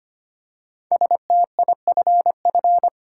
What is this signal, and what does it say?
Morse code, heard as a keyed tone, about 25 wpm, 700 Hz. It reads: STIFF
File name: 5 Letter Words - First 3 Letters Given - 25wpm